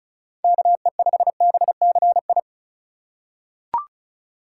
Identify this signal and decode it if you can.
Morse code, heard as a keyed tone, about 36 wpm, 700 Hz. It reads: KE5BCI
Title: USA Call Signs - 6 Characters - 4x Character Spacing - 35wpm